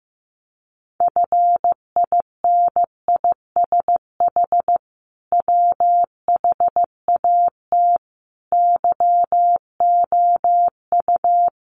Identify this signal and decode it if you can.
Morse code, heard as a keyed tone, about 15 wpm, 700 Hz. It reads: FINISH WHAT YOU
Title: Sentences from Top 1200 Words 15wpm